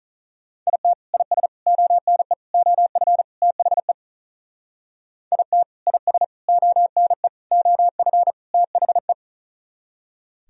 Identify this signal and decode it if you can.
Morse code, heard as a keyed tone, about 42 wpm, 700 Hz. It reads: IT IS ONE OF THE IT IS ONE OF THE
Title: Speed-Racing - Sets of 5 Words 35wpm